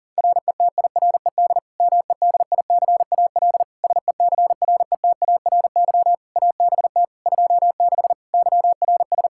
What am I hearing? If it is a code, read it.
Morse code, heard as a keyed tone, about 40 wpm, 700 Hz. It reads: RETIRED MEDICAL SECRETARY ABT 26 YRS
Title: K1USN SST QSO Practice 40wpm